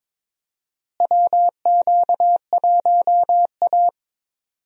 Morse code, heard as a keyed tone, about 22 wpm, 700 Hz. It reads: WQ1A
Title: USA Call Signs - 4 Characters - Repeated 2x - 22wpm